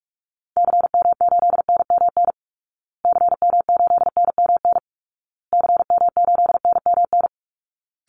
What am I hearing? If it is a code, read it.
Morse code, heard as a keyed tone, about 46 wpm, 700 Hz. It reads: CM8NMN CM8NMN CM8NMN
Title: Call Signs Repeated 2x - 45wpm